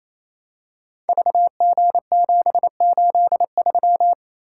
Morse code, heard as a keyed tone, about 28 wpm, 700 Hz. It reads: VG783
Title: Code Groups - 5 Letter:Numbers - 28wpm